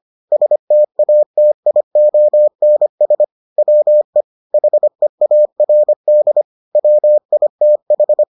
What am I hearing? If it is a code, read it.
Morse code, heard as a keyed tone, about 25 wpm, 600 Hz. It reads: STATIONS WE HEARD WITH